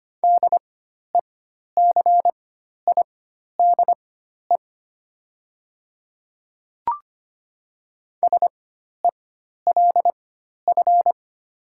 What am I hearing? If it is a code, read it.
Morse code, heard as a keyed tone, about 25 wpm, spaced as for 12 wpm, 700 Hz. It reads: DECIDE SELF
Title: 4x Character Spacing - Top 500 Words - 25wpm